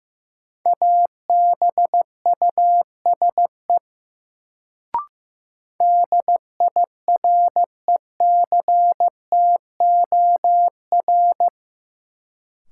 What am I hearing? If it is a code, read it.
Morse code, heard as a keyed tone, about 15 wpm, 700 Hz. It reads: ABUSE DIRECTOR
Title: Sentences from Top 1300 Words - New Word Drill - 15wpm